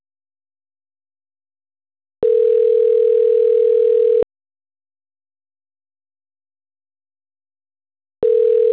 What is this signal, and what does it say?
Ringback tone (North America), 440 + 480 Hz.
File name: autotestagent / 11-57-37.136..callcentertransfer. .b54041f0-0afb-4db6-80f8